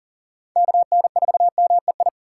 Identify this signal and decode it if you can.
Morse code, heard as a keyed tone, about 40 wpm, 700 Hz. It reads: KN4MEI
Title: USA Call Signs - 6 Characters - 2x Character Spacing - 40wpm